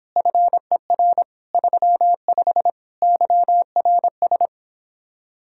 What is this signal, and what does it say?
Morse code, heard as a keyed tone, about 29 wpm, 700 Hz. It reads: FER 35 YRS